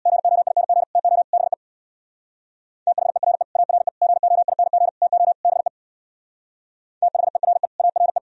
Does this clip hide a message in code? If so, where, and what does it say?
Yes: Morse code from 0.1 to 8.2 s, heard as a keyed tone, about 64 wpm, 700 Hz. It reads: GOING TO BE THERE ARE GOING TO BE THERE ARE